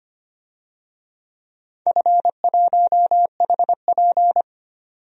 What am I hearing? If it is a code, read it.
Morse code, heard as a keyed tone, about 25 wpm, 700 Hz. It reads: F1HP
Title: DXCC-100 Call Signs - 4 Characters - 25wpm